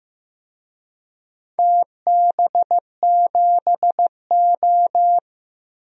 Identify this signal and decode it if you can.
Morse code, heard as a keyed tone, about 15 wpm, 700 Hz. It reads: TB7O
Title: DXCC-100 Call Signs - 4 Characters - Repeated 2x - 15wpm